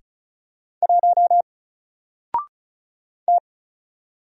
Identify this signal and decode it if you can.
Morse code, heard as a keyed tone, about 36 wpm, 700 Hz. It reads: 1 T
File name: Single-Letter-Number-35wpm